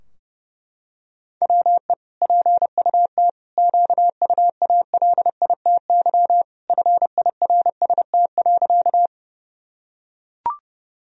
Morse code, heard as a keyed tone, about 30 wpm, 700 Hz. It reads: WE PUT QUALITY FIRST.
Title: Sentences from Top 1000 Words - Review Part 4 - 30wpm